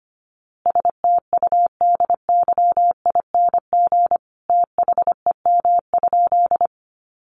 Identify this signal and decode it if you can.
Morse code, heard as a keyed tone, about 25 wpm, 700 Hz. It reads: STUDYING THEM?
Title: A Sound Of Thunder 25wpm